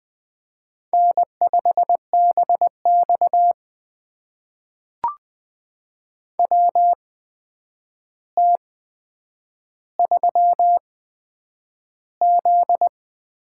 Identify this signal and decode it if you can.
Morse code, heard as a keyed tone, about 20 wpm, spaced as for 9 wpm, 700 Hz. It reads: N5BX WT3Z
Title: USA Call Signs - 4 Characters - 8x Character Spacing - 20wpm